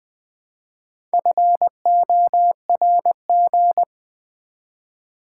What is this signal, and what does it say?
Morse code, heard as a keyed tone, about 20 wpm, 700 Hz. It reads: FORG
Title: Build A Word - 1 Letter At A Time - 3 to 5-Letter Words - 20wpm